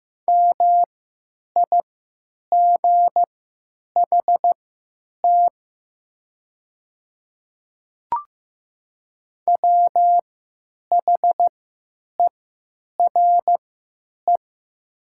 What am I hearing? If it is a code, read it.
Morse code, heard as a keyed tone, about 15 wpm, spaced as for 9 wpm, 700 Hz. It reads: MIGHT WHERE
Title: 3x Character Spacing - Top 300 Words - 15wpm